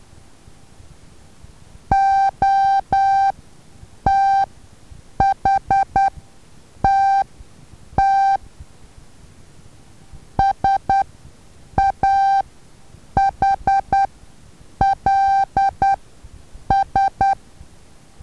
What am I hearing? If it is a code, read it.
Morse code, heard as a keyed tone, about 10 wpm, spaced as for 7 wpm, 780 Hz. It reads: OTHTT SAHLS
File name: Taviro HA6VA B